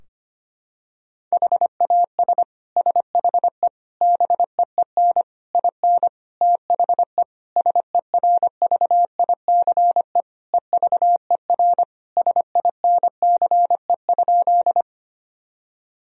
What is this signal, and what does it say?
Morse code, heard as a keyed tone, about 25 wpm, 700 Hz. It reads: HAS SHE BEEN IN THE SERVICE EVER SINCE?
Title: Sentences from Top 1000 Words - Review Part 4 - 25wpm